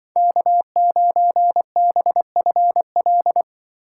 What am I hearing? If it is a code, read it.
Morse code, heard as a keyed tone, about 23 wpm, 700 Hz. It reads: K9BFL
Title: Speed Racing - USA 1x3 Call Signs 20wpm